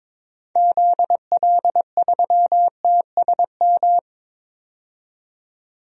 Morse code, heard as a keyed tone, about 22 wpm, 700 Hz. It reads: ZL3TSM
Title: DXCC-100 Call Signs - 6 Characters - 22wpm